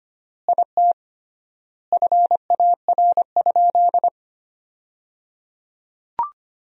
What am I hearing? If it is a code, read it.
Morse code, heard as a keyed tone, about 25 wpm, 700 Hz. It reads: IT FAR?
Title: 3x Word-Spacing - Sentences from Top 300 Words - 25wpm